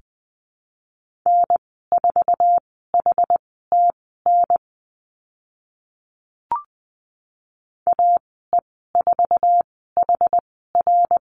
Morse code, heard as a keyed tone, about 20 wpm, spaced as for 15 wpm, 700 Hz. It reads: N4HTN AE4HR
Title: 069 Lesson 12g - Letter d - Call Signs 20wpm